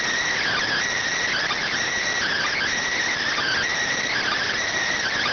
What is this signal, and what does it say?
Emergency Alert System (SAME) tones (header bursts), heard as harsh buzzing bursts. It is heard from 0.3 to 4.5 s.